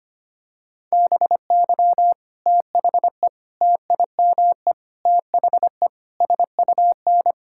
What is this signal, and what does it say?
Morse code, heard as a keyed tone, about 25 wpm, 700 Hz. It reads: BY THE TIME THE SUN
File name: practice / 25wpm / Sentences with 4-Word Phrases - 25wpm